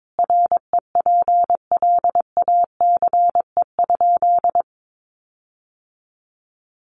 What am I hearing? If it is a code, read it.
Morse code, heard as a keyed tone, about 22 wpm, 700 Hz. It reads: REPLACE?
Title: Sentences from Top 1400 Words 22wpm